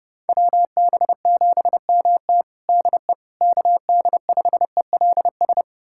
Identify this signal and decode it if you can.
Morse code, heard as a keyed tone, about 32 wpm, 700 Hz. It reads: WB7MT DE KD5ELS
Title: USA Callsign DE Callsign 30wpm